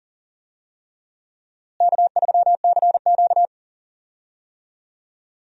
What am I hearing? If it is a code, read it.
Morse code, heard as a keyed tone, about 40 wpm, 700 Hz. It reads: K3CQ
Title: USA Call Signs - 4 Characters - Repeated 3x - 40wpm